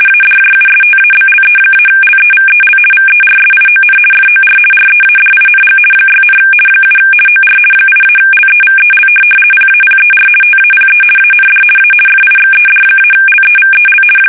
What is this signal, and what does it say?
RTTY, about 50 baud. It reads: ADJUSTABLE-LEVEL VERSION FOR READING 6, 7 OR 8-LEVEL PAPER TAPE AND A FIXED-LEVEL
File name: fsk850-50B / teletype-corp